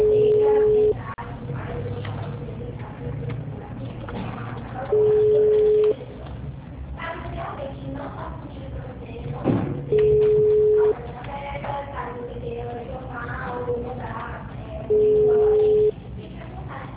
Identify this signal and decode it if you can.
Ringback tone (Europe, 425 Hz), with speech.